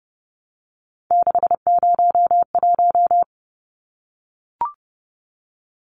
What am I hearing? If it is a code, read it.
Morse code, heard as a keyed tone, about 30 wpm, 700 Hz. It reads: 601
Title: Numbers - 3-Digit 30wpm